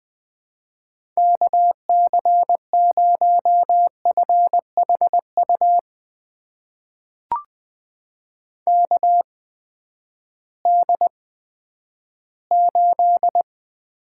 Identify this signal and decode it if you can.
Morse code, heard as a keyed tone, about 20 wpm, spaced as for 11 wpm, 700 Hz. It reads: KC0FHU KD8
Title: USA Call Signs - 6 Characters - 8x Character Spacing - 20wpm